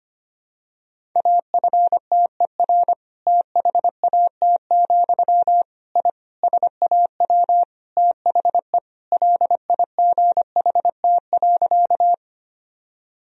Morse code, heard as a keyed tone, about 25 wpm, 700 Hz. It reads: AFTER THAT, I SAW THE LIGHT.
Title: Sentences from Top 1000 Words - Review Part 1 - 25wpm